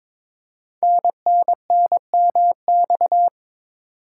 Morse code, heard as a keyed tone, about 22 wpm, 700 Hz. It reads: NNNMX
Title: Code Groups - 5 Letters - 22wpm